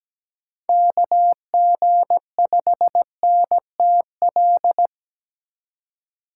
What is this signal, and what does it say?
Morse code, heard as a keyed tone, about 17 wpm, 700 Hz. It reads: KG5NTL
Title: USA Call Signs - 6 Characters - 2x Character Spacing - 17wpm